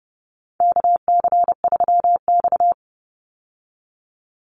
Morse code, heard as a keyed tone, about 30 wpm, 700 Hz. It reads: KC3X
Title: Call Signs Repeated 2x - 30wpm